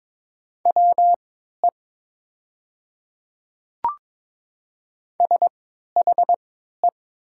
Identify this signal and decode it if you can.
Morse code, heard as a keyed tone, about 22 wpm, spaced as for 12 wpm, 700 Hz. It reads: WE SHE
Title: 3x Character Spacing - Top 200 Words - 22wpm